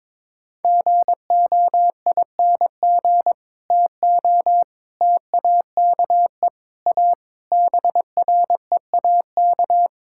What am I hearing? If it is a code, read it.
Morse code, heard as a keyed tone, about 22 wpm, 700 Hz. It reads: GOING TO TAKE A BREAK
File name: practice / 22wpm / Build A Phrase - 5-Word Phrases - 22wpm